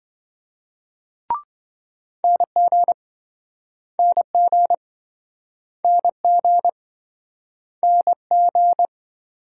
Morse code, heard as a keyed tone, about 28 wpm, 700 Hz. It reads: NG NG NG NG